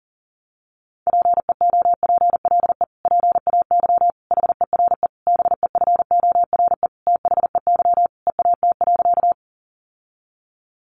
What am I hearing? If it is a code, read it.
Morse code, heard as a keyed tone, about 40 wpm, 700 Hz. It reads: PEOPLE PAY HERE BEFORE THEY EAT.
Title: Sentences from Top 700 Words 40wpm